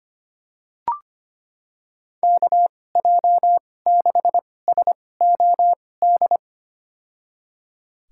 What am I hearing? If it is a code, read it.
Morse code, heard as a keyed tone, about 25 wpm, spaced as for 18 wpm, 700 Hz. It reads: KJ6SOD
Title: USA Call Signs - 6 Characters - 2x Character Spacing - 25wpm